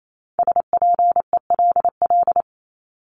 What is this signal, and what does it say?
Morse code, heard as a keyed tone, about 28 wpm, 700 Hz. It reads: SPELL
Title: Top 500 Words Repeated 28wpm